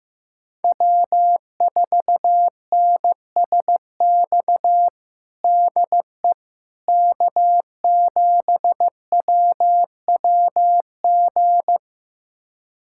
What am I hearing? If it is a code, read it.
Morse code, heard as a keyed tone, about 15 wpm, 700 Hz. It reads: W4NSX DE K7WWG
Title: USA Callsign DE Callsign 15wpm